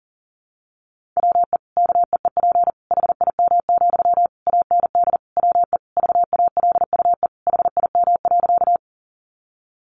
Morse code, heard as a keyed tone, about 40 wpm, 700 Hz. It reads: WE KEEP HIM, AND WE VALUE HIM.
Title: Sentences from Top 600 Words 40wpm